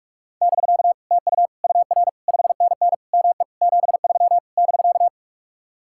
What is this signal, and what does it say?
Morse code, heard as a keyed tone, about 49 wpm, 700 Hz. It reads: <BK> TU UR 5NN ME 73 <BK>